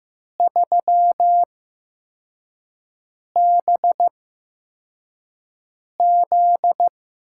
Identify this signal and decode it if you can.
Morse code, heard as a keyed tone, about 15 wpm, spaced as for 4 wpm, 700 Hz. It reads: 3BZ
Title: USA Call Signs - 6 Characters - 8x Character Spacing - 15wpm